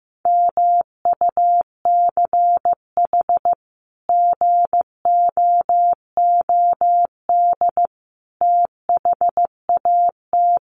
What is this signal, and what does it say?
Morse code, heard as a keyed tone, about 15 wpm, 700 Hz. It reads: MUCH GOOD THAT
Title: Sentences from Top 200 Words 15wpm